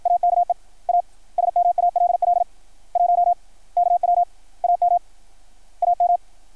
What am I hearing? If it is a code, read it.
Morse code, heard as a keyed tone, about 53 wpm, 690 Hz. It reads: AKE A SMALL 1 KW AM AM
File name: archived / CRSnet